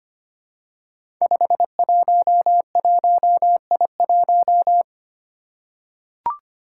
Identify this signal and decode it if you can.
Morse code, heard as a keyed tone, about 25 wpm, 700 Hz. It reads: 511I1
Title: Five Challenging Letter-Number Combinations 25wpm